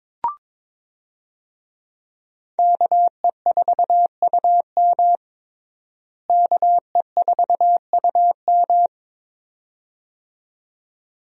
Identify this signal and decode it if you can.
Morse code, heard as a keyed tone, about 22 wpm, 700 Hz. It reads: KE4UM KE4UM
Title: USA Call Signs - 5 Characters - Repeated 2x - 22wpm